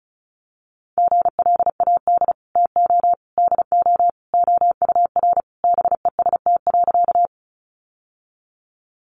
Morse code, heard as a keyed tone, about 34 wpm, 700 Hz. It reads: GLAD TO DO OUR BEST.
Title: Sentences from Top 800 Words 35wpm